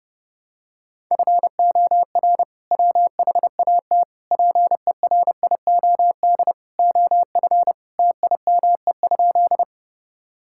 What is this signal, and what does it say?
Morse code, heard as a keyed tone, about 32 wpm, 700 Hz. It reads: FOR WHAT PERIOD OF TIME?
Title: Sentences from Top 1000 Words - Review Part 5 - 30wpm